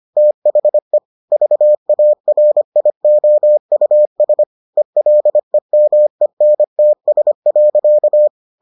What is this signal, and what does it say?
Morse code, heard as a keyed tone, about 25 wpm, 600 Hz. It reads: THE VARIOUS ELEMENTS.